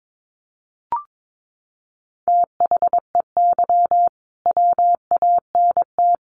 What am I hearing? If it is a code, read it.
Morse code, heard as a keyed tone, about 22 wpm, 700 Hz. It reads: THEY WANT
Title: Sets of 3 Words 22wpm